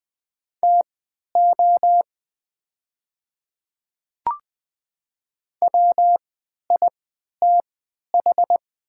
Morse code, heard as a keyed tone, about 20 wpm, spaced as for 11 wpm, 700 Hz. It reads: TO WITH
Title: 3x Character Spacing - Top 100 Words - 20wpm